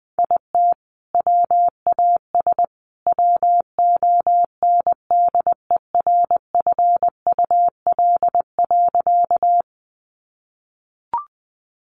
Morse code, heard as a keyed tone, about 20 wpm, 700 Hz. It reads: IT WAS WONDERFUL.
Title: Sentences from Top 1100 Words 20wpm